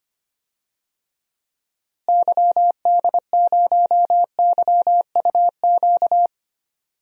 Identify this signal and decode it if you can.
Morse code, heard as a keyed tone, about 25 wpm, 700 Hz. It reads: YD0YUQ
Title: DXCC-100 Call Signs - 6 Characters - Repeated 2x - 25wpm